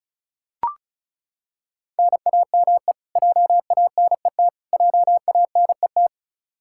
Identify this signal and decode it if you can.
Morse code, heard as a keyed tone, about 36 wpm, 700 Hz. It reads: NAME JANET JANET
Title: Practice QSO - Exchange 1 - Beginning - 35wpm